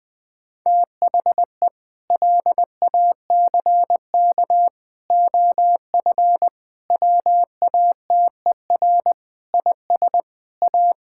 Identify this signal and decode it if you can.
Morse code, heard as a keyed tone, about 20 wpm, 700 Hz. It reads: THE LACK OF WATER IS A
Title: Sentences with 3-Word Phrases - Spoken Beginning - 20wpm